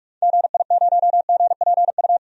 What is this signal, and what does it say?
Morse code, heard as a keyed tone, about 42 wpm, 700 Hz. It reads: GI0GPU
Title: DXCC-100 Call Signs - 6 Characters - 45wpm